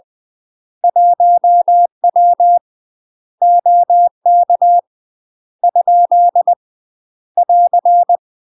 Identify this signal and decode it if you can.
Morse code, heard as a keyed tone, about 20 wpm, 700 Hz. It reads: 1W OK ? <AR>
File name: UK MASTER QSOs TRAINING / 20WPM / UK MASTER QSOs 0003